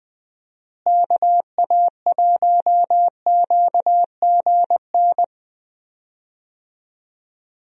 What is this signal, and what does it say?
Morse code, heard as a keyed tone, about 20 wpm, 700 Hz. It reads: KA1QGN
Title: USA Call Signs - 6 Characters - Repeated 3x - 20wpm